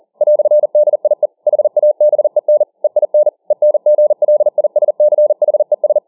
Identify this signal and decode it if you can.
Morse code, heard as a keyed tone, about 40 wpm, 600 Hz. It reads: "DIE HABEN EIN ENGLISCHES